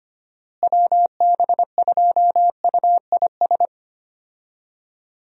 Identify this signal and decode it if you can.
Morse code, heard as a keyed tone, about 25 wpm, 700 Hz. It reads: WB2UIS
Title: USA Call Signs - 6 Characters - Repeated 2x - 25wpm